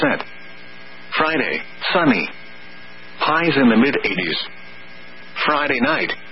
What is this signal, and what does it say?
Emergency Alert System (SAME) tones (header bursts), heard as harsh buzzing bursts, with speech. It is heard from 0.3 to 5.3 s.